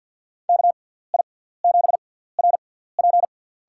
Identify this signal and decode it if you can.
Morse code, heard as a keyed tone, about 49 wpm, spaced as for 17 wpm, 700 Hz. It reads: KI7RP
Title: USA Call Signs - 6x Character Spacing - 50wpm